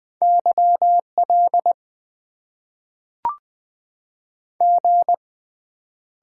Morse code, heard as a keyed tone, about 20 wpm, 700 Hz. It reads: YL G
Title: Top 90 QSO Elements - 6x Character Spacing - 20wpm